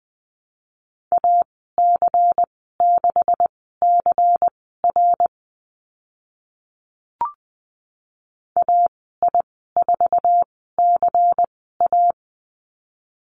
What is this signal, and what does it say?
Morse code, heard as a keyed tone, about 20 wpm, spaced as for 14 wpm, 700 Hz. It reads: AC6CR AI4CA
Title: 154 Lesson 25g - Letter p - Call Signs 20wpm